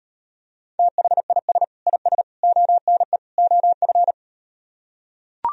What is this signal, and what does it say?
Morse code, heard as a keyed tone, about 38 wpm, 700 Hz. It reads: THIS IS ONE OF